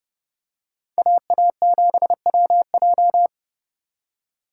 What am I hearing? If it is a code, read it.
Morse code, heard as a keyed tone, about 31 wpm, 700 Hz. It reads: AA7WJ